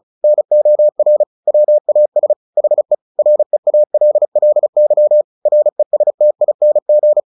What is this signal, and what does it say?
Morse code, heard as a keyed tone, about 36 wpm, 600 Hz. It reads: NOR WAS HE REALLY RESTING
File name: the old man and the sea 2 35wpm